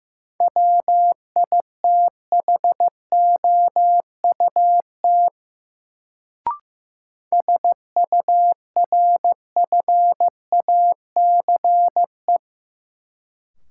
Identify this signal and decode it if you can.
Morse code, heard as a keyed tone, about 15 wpm, 700 Hz. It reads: WITHOUT SURFACE
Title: Sentences from Top 600 Words - New Word Drill - 15wpm